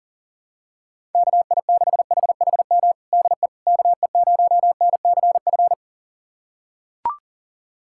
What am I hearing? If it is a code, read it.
Morse code, heard as a keyed tone, about 40 wpm, 700 Hz. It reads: KI6HHM DE KE0NCF
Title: USA Callsign DE Callsign 40wpm